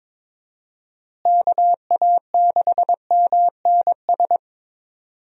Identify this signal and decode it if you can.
Morse code, heard as a keyed tone, about 22 wpm, 700 Hz. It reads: KA6MNS
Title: USA Call Signs - 6 Characters - Repeated 3x - 22wpm